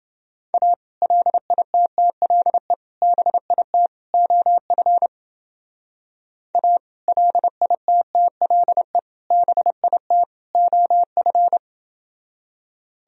Morse code, heard as a keyed tone, about 28 wpm, 700 Hz. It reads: A LITTLE BIT OF A LITTLE BIT OF